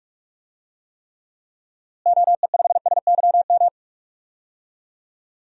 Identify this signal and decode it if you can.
Morse code, heard as a keyed tone, about 44 wpm, 700 Hz. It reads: OE5SYM
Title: DXCC-100 Call Signs - 6 Characters - Repeated 2x - 45wpm